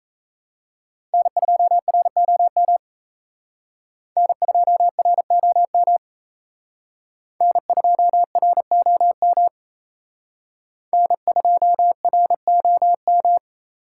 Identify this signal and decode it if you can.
Morse code, heard as a keyed tone, about 36 wpm, 700 Hz. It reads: N2ROM N2ROM N2ROM N2ROM